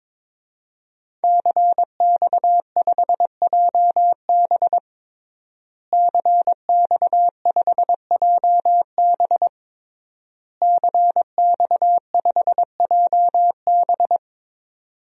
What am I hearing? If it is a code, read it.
Morse code, heard as a keyed tone, about 22 wpm, 700 Hz. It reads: CX5JB CX5JB CX5JB
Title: DXCC-100 Call Signs - 5 Characters - Repeated 3x - 22wpm